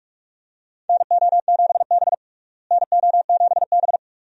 Morse code, heard as a keyed tone, about 41 wpm, 700 Hz. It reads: NO7B NO7B
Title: USA Call Signs - 4 Characters - Repeated 2x - 45wpm